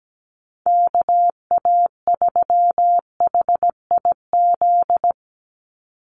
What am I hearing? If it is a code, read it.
Morse code, heard as a keyed tone, about 17 wpm, 700 Hz. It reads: KA3HIZ
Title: USA Call Signs 17wpm